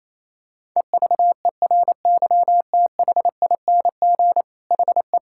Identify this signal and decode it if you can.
Morse code, heard as a keyed tone, about 28 wpm, 700 Hz. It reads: EVERYTHING HE
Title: Sentences from Top 1000 Words - Review Part 6 - 28wpm